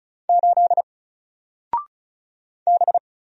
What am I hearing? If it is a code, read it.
Morse code, heard as a keyed tone, about 35 wpm, 700 Hz. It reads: 8 B
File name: Single-Letter-Number-35wpm